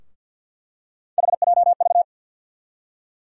Morse code, heard as a keyed tone, about 51 wpm, 700 Hz. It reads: HJV
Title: USA Call Sign Postfixes - 3 Letters 50wpm